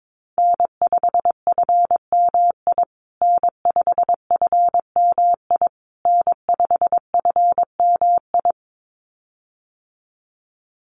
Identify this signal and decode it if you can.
Morse code, heard as a keyed tone, about 22 wpm, 700 Hz. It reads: N5FMI N5FMI N5FMI
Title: USA Call Signs Repeated 3x 22wpm